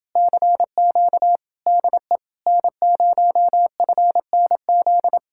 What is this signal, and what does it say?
Morse code, heard as a keyed tone, about 29 wpm, 700 Hz. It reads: CQ DE N0FNZ